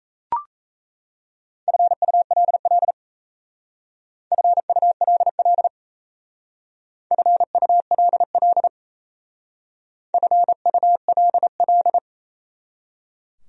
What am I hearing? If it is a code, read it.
Morse code, heard as a keyed tone, about 37 wpm, 700 Hz. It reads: FULL FULL FULL FULL